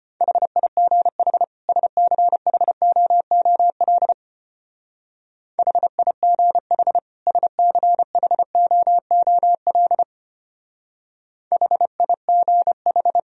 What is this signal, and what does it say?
Morse code, heard as a keyed tone, about 29 wpm, 700 Hz. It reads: HIGH SCHOOL HIGH SCHOOL HIGH
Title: Speed-Racing - Sets of 2 Words 25wpm